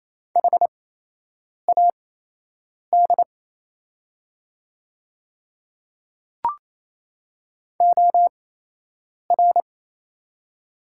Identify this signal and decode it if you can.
Morse code, heard as a keyed tone, about 28 wpm, spaced as for 8 wpm, 700 Hz. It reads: HAD OR
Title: 8x Character Spacing - Top 100 Words - 28wpm